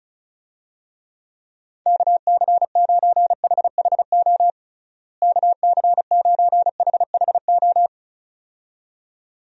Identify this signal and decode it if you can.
Morse code, heard as a keyed tone, about 36 wpm, 700 Hz. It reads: KC9HHO KC9HHO
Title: USA Call Signs - 6 Characters - Repeated 2x - 35wpm